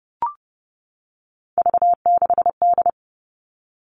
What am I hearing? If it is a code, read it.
Morse code, heard as a keyed tone, about 30 wpm, 700 Hz. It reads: V6D
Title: All Cut Numbers - 3-Digit 30wpm